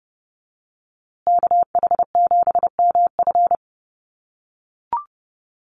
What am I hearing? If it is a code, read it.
Morse code, heard as a keyed tone, about 30 wpm, 700 Hz. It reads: KH7MF
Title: USA Call Signs Repeated 2x 30wpm